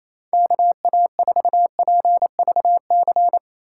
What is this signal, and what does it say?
Morse code, heard as a keyed tone, about 28 wpm, 700 Hz. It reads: KA4PVC
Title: USA Call Signs - 6 Characters - 4x Character Spacing - 28wpm